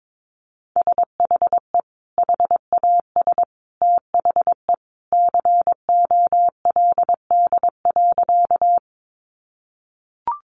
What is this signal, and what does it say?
Morse code, heard as a keyed tone, about 22 wpm, 700 Hz. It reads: SHE HAS THE COLD.
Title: Easy Phrases 22wpm